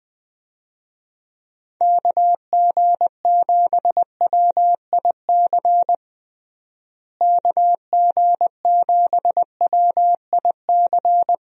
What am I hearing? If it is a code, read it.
Morse code, heard as a keyed tone, about 20 wpm, 700 Hz. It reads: KG7WIC KG7WIC
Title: USA Call Signs - 6 Characters - Repeated 2x - 20wpm